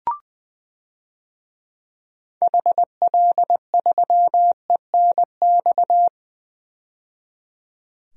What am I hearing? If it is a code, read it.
Morse code, heard as a keyed tone, about 21 wpm, 700 Hz. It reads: HL3ENX
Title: DXCC-100 Call Signs - 6 Characters - 20wpm